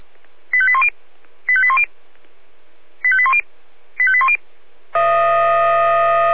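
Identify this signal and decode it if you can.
Five-tone selective call (ZVEI), heard as quick runs of five tones, tones of about 70 ms. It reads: 85810 85810 85810 85810
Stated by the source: SDR receiver, ZVEI five-tone selective call, address 85310.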